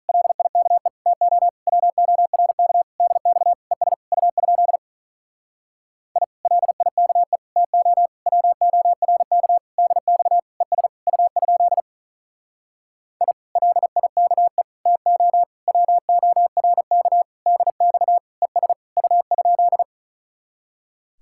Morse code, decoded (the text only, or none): LIKE TO WORK DX ES U? I LIKE TO WORK DX ES U? I LIKE TO WORK DX ES U?